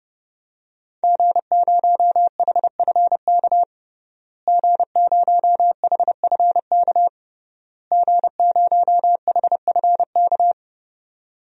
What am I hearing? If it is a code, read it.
Morse code, heard as a keyed tone, about 31 wpm, 700 Hz. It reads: G0HFK G0HFK G0HFK
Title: DXCC-100 Call Signs - 5 Characters - Repeated 3x - 30wpm